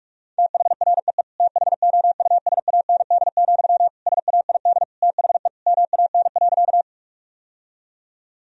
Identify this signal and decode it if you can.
Morse code, heard as a keyed tone, about 46 wpm, 700 Hz. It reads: THREE THOUSAND, SAID THE MAN.
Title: Sentences from Top 1000 Words - Review Part 3 - 45wpm